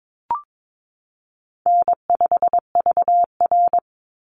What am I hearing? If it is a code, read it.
Morse code, heard as a keyed tone, about 22 wpm, 700 Hz. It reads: N5VR
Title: Easy Call Signs - 22wpm